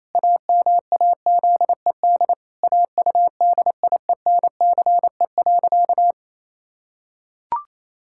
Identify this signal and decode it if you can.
Morse code, heard as a keyed tone, about 28 wpm, 700 Hz. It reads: AMAZED AUDIENCE.
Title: Pangram Challenge v2 - 28wpm